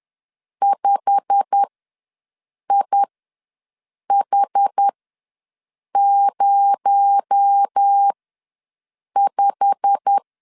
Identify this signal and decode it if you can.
Morse code, heard as a keyed tone, about 10 wpm, spaced as for 6 wpm, 800 Hz. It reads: 5IH05
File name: LEZIONI / 8 YJ45 / 8 YJ45 ws 10wpm 3843